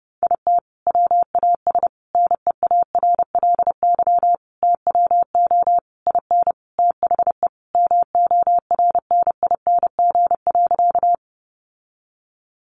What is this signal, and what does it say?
Morse code, heard as a keyed tone, about 28 wpm, 700 Hz. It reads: IT WAS NEARLY TWO IN THE MORNING.
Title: Sentences from Top 1300 Words 30wpm